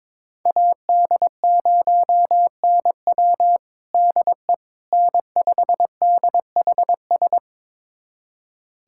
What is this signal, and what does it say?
Morse code, heard as a keyed tone, about 22 wpm, 700 Hz. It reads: AD0NW DE N5DHS
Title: USA Callsign DE Callsign - 5 Characters - 22wpm